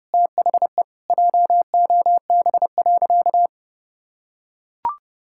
Morse code, heard as a keyed tone, about 30 wpm, 700 Hz. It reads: THE JOB.
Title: Sentences from Top 1000 Words - Review Part 5 - 30wpm